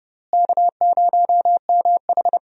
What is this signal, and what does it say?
Morse code, heard as a keyed tone, about 30 wpm, 700 Hz. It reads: K0MH
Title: USA Call Signs - 4 Characters - Repeated 3x - 30wpm